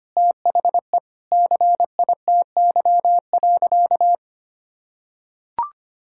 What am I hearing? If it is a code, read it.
Morse code, heard as a keyed tone, about 25 wpm, 700 Hz. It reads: THE CITY.
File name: practice / 25wpm / Sentences with 4-Word Phrases - 25wpm